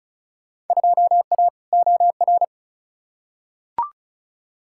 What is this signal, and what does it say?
Morse code, heard as a keyed tone, about 36 wpm, 700 Hz. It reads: 2A OR
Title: Field Day Exchanges - 35wpm